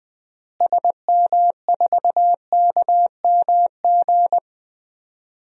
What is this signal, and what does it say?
Morse code, heard as a keyed tone, about 21 wpm, 700 Hz. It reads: SM4KMG
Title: DXCC-100 Call Signs - 6 Characters - Repeated 3x - 20wpm